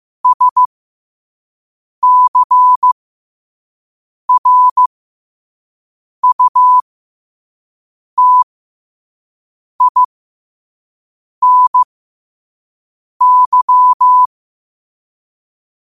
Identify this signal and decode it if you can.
Morse code, heard as a keyed tone, about 15 wpm, spaced as for 5 wpm, 1000 Hz. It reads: SCRUTINY
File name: qotdfiles / 5wpm